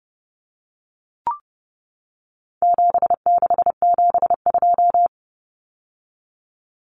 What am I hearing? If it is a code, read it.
Morse code, heard as a keyed tone, about 30 wpm, 700 Hz. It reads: 7672
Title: Common Cut Numbers - 4-Digit 30wpm